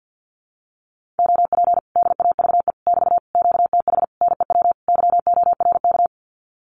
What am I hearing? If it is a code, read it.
Morse code, heard as a keyed tone, about 50 wpm, 700 Hz. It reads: OP DAVE <BT> QTH NEW YORK
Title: Common QSO Phrases Encore 50wpm